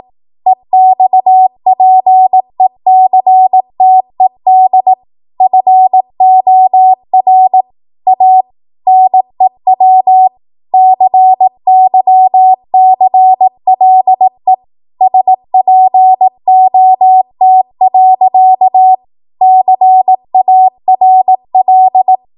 Morse code, heard as a keyed tone, about 18 wpm, 750 Hz. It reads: EXPECTED FOR A NEW CYCLE SPOT. CARL
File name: Archive / 18 WPM / 18WPM